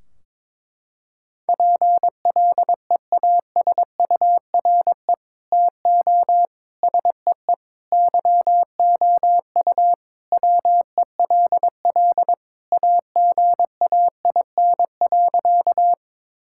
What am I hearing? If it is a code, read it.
Morse code, heard as a keyed tone, about 22 wpm, 700 Hz. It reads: PLEASURE TO SEE YOU WELL AGAIN.
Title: Sentences from Top 1600 Words 22wpm